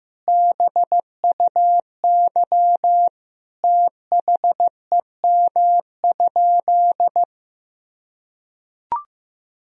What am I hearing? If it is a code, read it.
Morse code, heard as a keyed tone, about 16 wpm, 700 Hz. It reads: BUY THEM?
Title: Sentences from Top 1000 Words - Review Part 4 - 15wpm